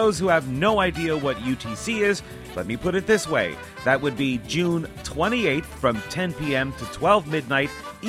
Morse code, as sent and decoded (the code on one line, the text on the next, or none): none